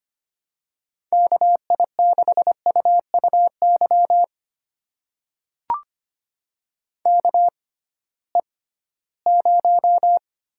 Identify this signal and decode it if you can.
Morse code, heard as a keyed tone, about 25 wpm, spaced as for 16 wpm, 700 Hz. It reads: KI6UUY KE0
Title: USA Call Signs - 6 Characters - 6x Character Spacing - 25wpm